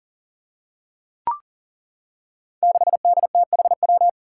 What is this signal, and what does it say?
Morse code, heard as a keyed tone, about 38 wpm, 700 Hz. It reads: 6DTHW
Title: Five Challenging Letter-Number Combinations 40wpm